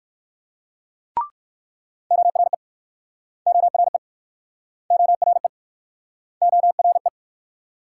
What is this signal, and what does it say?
Morse code, heard as a keyed tone, about 62 wpm, 690 Hz. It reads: ORE ORE ORE ORE